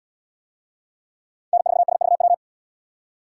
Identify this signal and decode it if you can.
Morse code, heard as a keyed tone, about 72 wpm, 700 Hz. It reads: N4SVW